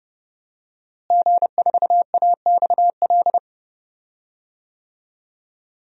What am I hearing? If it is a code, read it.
Morse code, heard as a keyed tone, about 32 wpm, 700 Hz. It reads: G4AXL
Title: DXCC-100 Call Signs - 5 Characters - Repeated 3x - 30wpm